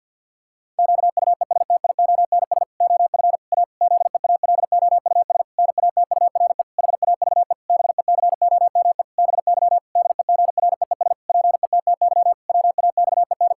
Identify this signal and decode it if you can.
Morse code, heard as a keyed tone, about 50 wpm, 700 Hz. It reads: QUESTIONS OF A ZEALOUS NATURE HAVE BECOME BY DEGREES PETTY WAXEN
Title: Pangram Challenge v2 - 50wpm